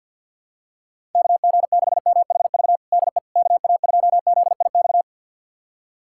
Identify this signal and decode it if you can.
Morse code, heard as a keyed tone, about 48 wpm, 700 Hz. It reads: KG6MHV DE KA2ZIX
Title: USA Callsign DE Callsign 50wpm